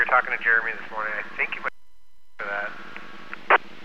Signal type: Emergency Alert System (SAME) tones